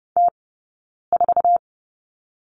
Morse code, heard as a keyed tone, about 30 wpm, 700 Hz. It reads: T 4
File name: Lesson 17d - Letter u - Two Characters 30wpm